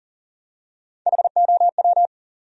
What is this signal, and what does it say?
Morse code, heard as a keyed tone, about 37 wpm, 700 Hz. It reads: HOW
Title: Top 100 Words 40wpm